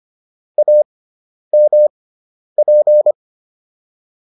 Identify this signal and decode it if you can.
Morse code, heard as a keyed tone, about 25 wpm, spaced as for 10 wpm, 600 Hz. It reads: AMP